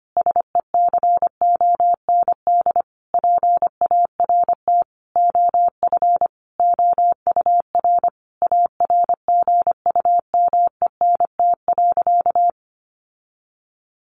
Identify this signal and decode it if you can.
Morse code, heard as a keyed tone, about 25 wpm, 700 Hz. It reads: SECOND PART OF OUR ARGUMENT.
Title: Sentences from Top 1400 Words 25wpm